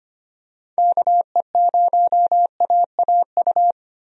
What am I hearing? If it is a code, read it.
Morse code, heard as a keyed tone, about 25 wpm, 700 Hz. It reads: KE0AAU
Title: USA Call Signs - 6 Characters - 25wpm